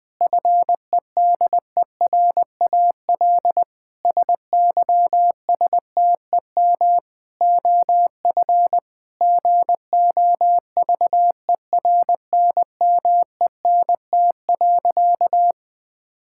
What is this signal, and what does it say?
Morse code, heard as a keyed tone, about 21 wpm, 700 Hz. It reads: FEDERAL SYSTEM OF GOVERNMENT.
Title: Sentences from Top 1000 Words - Review Part 5 - 20wpm